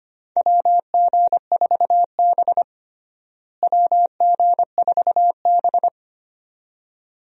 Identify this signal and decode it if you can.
Morse code, heard as a keyed tone, about 25 wpm, 700 Hz. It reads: WG4B WG4B
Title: USA Call Signs - 4 Characters - Repeated 2x - 25wpm